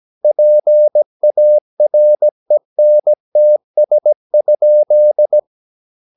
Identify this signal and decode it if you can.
Morse code, heard as a keyed tone, about 17 wpm, 600 Hz. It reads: PARENTS?
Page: Romanen Never af Ken Follet på engelsk.